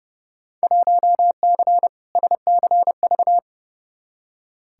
Morse code, heard as a keyed tone, about 30 wpm, 700 Hz. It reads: 1C SCV
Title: Field Day Exchanges - 30wpm